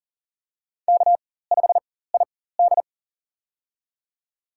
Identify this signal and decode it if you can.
Morse code, heard as a keyed tone, about 40 wpm, spaced as for 19 wpm, 700 Hz. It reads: K5ID
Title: USA Call Signs - 4 Characters - 4x Character Spacing - 40wpm